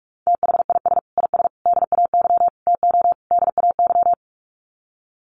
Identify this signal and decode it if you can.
Morse code, heard as a keyed tone, about 44 wpm, 700 Hz. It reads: THIS IS DAY TO DAY
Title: Sets of 5 Words Encore 45wpm